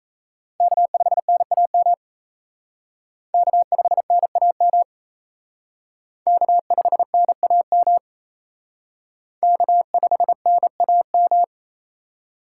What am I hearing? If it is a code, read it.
Morse code, heard as a keyed tone, about 34 wpm, 700 Hz. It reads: K5NAM K5NAM K5NAM K5NAM
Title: Speed Racing - USA 1x3 Call Signs 28wpm